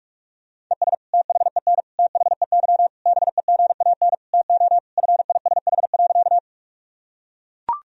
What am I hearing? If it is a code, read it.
Morse code, heard as a keyed tone, about 41 wpm, 700 Hz. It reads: ES THEN THEY BEGAN TO FISH.
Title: Sentences from Top 300 Words and Abbreviations 45wpm